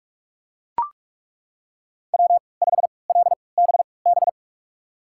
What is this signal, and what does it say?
Morse code, heard as a keyed tone, about 42 wpm, spaced as for 25 wpm, 700 Hz. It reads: W5LBB
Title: USA Call Signs - 3x Character Spacing - 45wpm